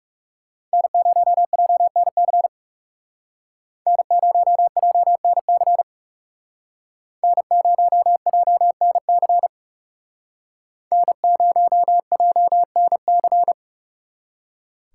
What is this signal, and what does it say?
Morse code, heard as a keyed tone, about 36 wpm, 700 Hz. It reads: N0JNC N0JNC N0JNC N0JNC
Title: Speed Racing - USA Call Signs 30wpm